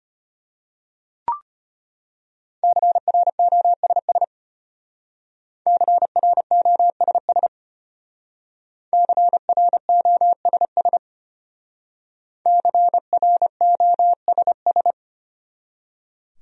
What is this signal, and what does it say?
Morse code, heard as a keyed tone, about 33 wpm, 700 Hz. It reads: CROSS CROSS CROSS CROSS